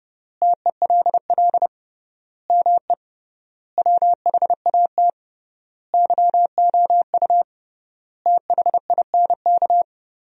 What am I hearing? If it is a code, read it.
Morse code, heard as a keyed tone, about 30 wpm, 700 Hz. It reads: TELL ME WHAT YOU THINK
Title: 3x Word-Spacing - Sentences from Top 200 Words - 30wpm